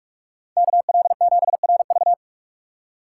Morse code, heard as a keyed tone, about 46 wpm, 700 Hz. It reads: KL7RV
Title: USA Call Signs - 5 Characters - 3x Character Spacing - 45wpm